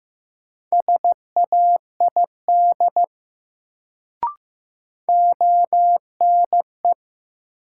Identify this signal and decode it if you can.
Morse code, heard as a keyed tone, about 15 wpm, 700 Hz. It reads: SAID ONE
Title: Sets of 2 Words - New Word Drill - 15wpm